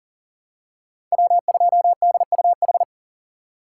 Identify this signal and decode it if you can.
Morse code, heard as a keyed tone, about 40 wpm, 700 Hz. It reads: W2DUH
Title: USA Call Signs - 5 Characters - 6x Character Spacing - 40wpm